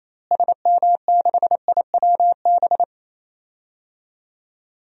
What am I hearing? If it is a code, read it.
Morse code, heard as a keyed tone, about 28 wpm, 700 Hz. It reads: SM6IWB
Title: DXCC-100 Call Signs - 6 Characters - Repeated 2x - 28wpm